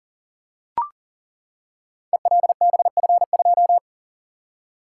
Morse code, heard as a keyed tone, about 40 wpm, 690 Hz. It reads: ELBF2
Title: Code Groups - 5 Letter:Numbers - 40wpm